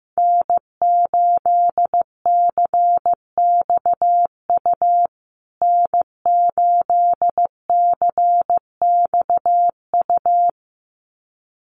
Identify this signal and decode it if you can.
Morse code, heard as a keyed tone, about 15 wpm, 700 Hz. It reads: N8CXU N8CXU
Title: USA Call Signs Repeated 2x 15wpm